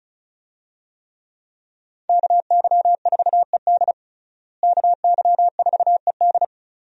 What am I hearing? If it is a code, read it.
Morse code, heard as a keyed tone, about 34 wpm, 700 Hz. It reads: KY4ED KY4ED
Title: USA Call Signs - 5 Characters - Repeated 2x - 35wpm